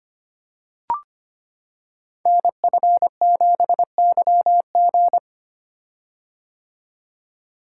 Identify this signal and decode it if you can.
Morse code, heard as a keyed tone, about 25 wpm, 700 Hz. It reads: NF7YG
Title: Code Groups - 5 Letter:Numbers - 25wpm